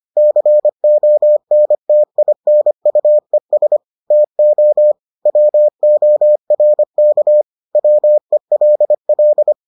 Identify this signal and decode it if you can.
Morse code, heard as a keyed tone, about 25 wpm, 600 Hz. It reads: CONTINUES TO WORK WELL